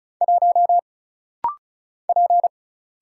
Morse code, heard as a keyed tone, about 36 wpm, 700 Hz. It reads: 1 P
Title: Single Letter-Number Alt - Rapid-Fire 35wpm